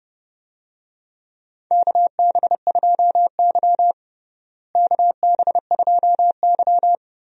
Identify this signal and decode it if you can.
Morse code, heard as a keyed tone, about 32 wpm, 700 Hz. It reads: KB2Y KB2Y
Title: USA Call Signs - 4 Characters - Repeated 3x - 30wpm